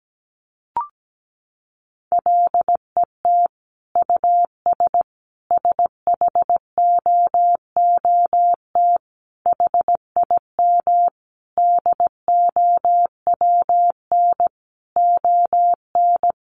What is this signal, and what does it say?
Morse code, heard as a keyed tone, about 17 wpm, 700 Hz. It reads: LET US SHOOT HIM DOWN ON
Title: Sentences from Top 1200 Words 17wpm